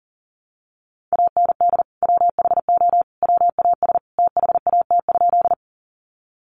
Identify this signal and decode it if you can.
Morse code, heard as a keyed tone, about 40 wpm, 700 Hz. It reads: AND WHO WAS THAT?
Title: Sentences from Top 100 Words 40wpm